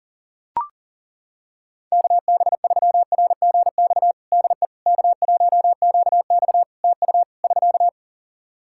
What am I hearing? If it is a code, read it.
Morse code, heard as a keyed tone, about 40 wpm, 700 Hz. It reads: KB3RGX DE K1QX TU <SK>
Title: Practice QSO - Exchange 4 - Ending - 40wpm